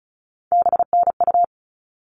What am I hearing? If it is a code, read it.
Morse code, heard as a keyed tone, about 34 wpm, 700 Hz. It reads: BNU
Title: All Cut Numbers - 3-Digit 35wpm